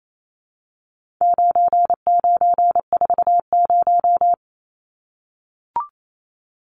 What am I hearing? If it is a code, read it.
Morse code, heard as a keyed tone, about 28 wpm, 700 Hz. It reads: 9940
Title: Numbers - 4-Digit 28wpm